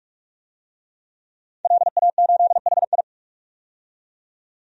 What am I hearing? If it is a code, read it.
Morse code, heard as a keyed tone, about 44 wpm, 700 Hz. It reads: LA8HI
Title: DXCC-100 Call Signs - 5 Characters - 45wpm